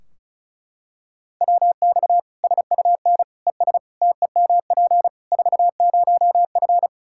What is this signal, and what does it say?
Morse code, heard as a keyed tone, about 36 wpm, 700 Hz. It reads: WX SUN ES TEMP 40F
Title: Practice QSO - Exchange 2 - Info - 35wpm